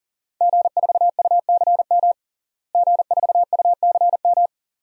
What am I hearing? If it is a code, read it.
Morse code, heard as a keyed tone, about 40 wpm, 700 Hz. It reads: G4UCM G4UCM
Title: DXCC-100 Call Signs - 5 Characters - Repeated 2x - 40wpm